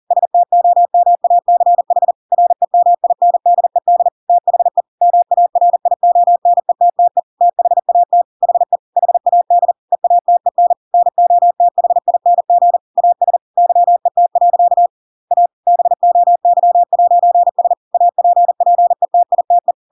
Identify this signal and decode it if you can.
Morse code, heard as a keyed tone, about 40 wpm, 700 Hz. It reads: STOMACH REMINDED THE MARIONETTE THAT HE HAD EATEN NOTHING AS YET. A BOY'S APPETITE
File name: Pinocchio / Pinocchio ch5 40 40 ik1hge